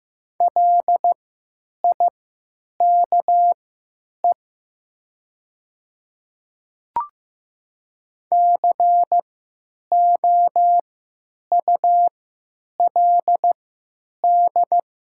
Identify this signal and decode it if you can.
Morse code, heard as a keyed tone, about 15 wpm, spaced as for 9 wpm, 700 Hz. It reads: LIKE COULD
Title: 3x Character Spacing - Top 200 Words - 15wpm